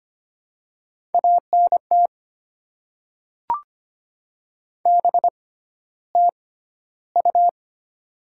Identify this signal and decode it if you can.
Morse code, heard as a keyed tone, about 25 wpm, spaced as for 13 wpm, 700 Hz. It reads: ANT BTU
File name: Top 20 QSO Elements - 6x Character Spacing - 25wpm